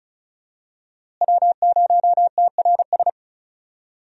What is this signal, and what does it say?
Morse code, heard as a keyed tone, about 34 wpm, 700 Hz. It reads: W0TRS
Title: USA Call Signs - 5 Characters - Repeated 2x - 35wpm